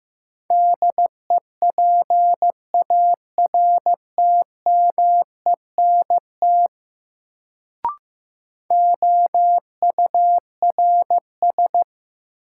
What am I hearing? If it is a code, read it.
Morse code, heard as a keyed tone, about 15 wpm, 700 Hz. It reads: DEPARTMENT OURS
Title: Sentences from Top 1000 Words - Review Part 5 - New Word Drill - 15wpm